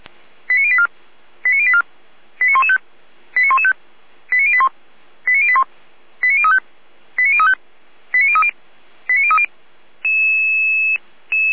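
Five-tone selective call (ZVEI), heard as quick runs of five tones, tones of about 70 ms. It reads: 89084 89084 89115 89115 89081 89081 89035 89035 89030 89030
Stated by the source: ZVEI five-tone call received on SDR by feuerwehr-orlamuende (fire brigade Orlamünde).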